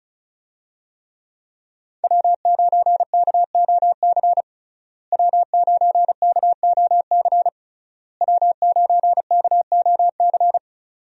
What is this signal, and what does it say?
Morse code, heard as a keyed tone, about 36 wpm, 700 Hz. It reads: W9KOC W9KOC W9KOC
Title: USA Call Signs - 5 Characters - Repeated 3x - 35wpm